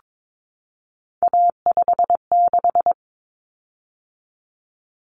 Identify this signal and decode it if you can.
Morse code, heard as a keyed tone, about 22 wpm, 700 Hz. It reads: A56
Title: Common Cut Numbers - 3-Digit 22wpm